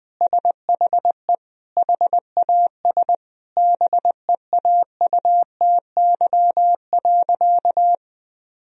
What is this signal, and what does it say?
Morse code, heard as a keyed tone, about 20 wpm, 700 Hz. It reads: SHE HAS BEAUTY.
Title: Sentences from Top 1000 Words - Review Part 2 - 20wpm